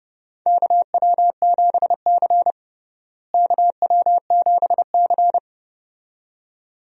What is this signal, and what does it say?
Morse code, heard as a keyed tone, about 30 wpm, 700 Hz. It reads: KW7C KW7C
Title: USA Call Signs - 4 Characters - Repeated 2x - 30wpm